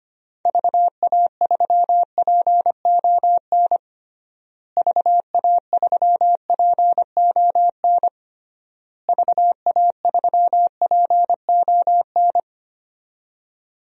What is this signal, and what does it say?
Morse code, heard as a keyed tone, about 25 wpm, 700 Hz. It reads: VA3PON VA3PON VA3PON
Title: DXCC-100 Call Signs - 6 Characters - Repeated 3x - 25wpm